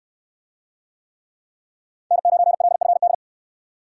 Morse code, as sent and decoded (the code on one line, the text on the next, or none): -. ----. .-.. ..-. -..
N9LFD